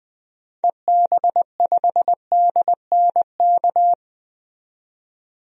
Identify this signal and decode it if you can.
Morse code, heard as a keyed tone, about 20 wpm, 700 Hz. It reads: EB5DNK
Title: DXCC-100 Call Signs - 6 Characters - 20wpm